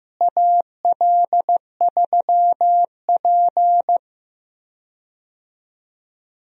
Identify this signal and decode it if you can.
Morse code, heard as a keyed tone, about 15 wpm, 700 Hz. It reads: AL3P
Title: USA Call Signs - 4 Characters - Repeated 3x - 15wpm